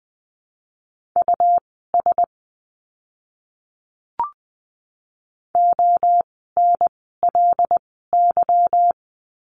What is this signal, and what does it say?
Morse code, heard as a keyed tone, about 20 wpm, spaced as for 14 wpm, 700 Hz. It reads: US ONLY
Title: 230 Lesson 37f - Letter x - Words 20wpm